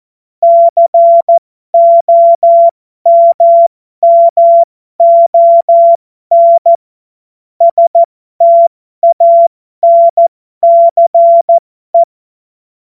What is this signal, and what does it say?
Morse code, heard as a keyed tone, about 14 wpm, spaced as for 12 wpm, 680 Hz. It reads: COMMON STANCE
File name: Morsekurs / Witze